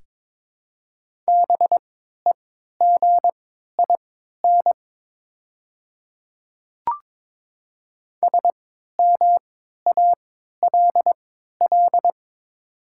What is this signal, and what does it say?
Morse code, heard as a keyed tone, about 22 wpm, spaced as for 13 wpm, 700 Hz. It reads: BEGIN SMALL
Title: 3x Character Spacing - Top 500 Words - 22wpm